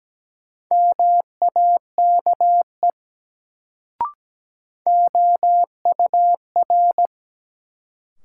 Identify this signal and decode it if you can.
Morse code, heard as a keyed tone, about 17 wpm, 700 Hz. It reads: MAKE OUR
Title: Sets of 2 Words - New Word Drill - 17wpm